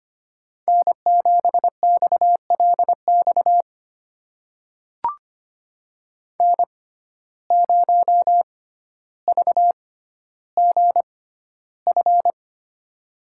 Morse code, heard as a keyed tone, about 25 wpm, spaced as for 13 wpm, 700 Hz. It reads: N7XLX N0VGF
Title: USA Call Signs - 5 Characters - 6x Character Spacing - 25wpm